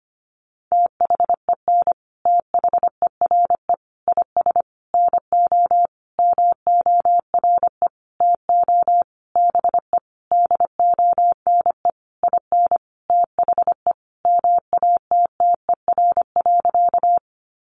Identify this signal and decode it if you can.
Morse code, heard as a keyed tone, about 25 wpm, 700 Hz. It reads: THEN THERE IS NO MORE TO BE DONE IN THE MATTER.
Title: Sentences from Top 700 Words 25wpm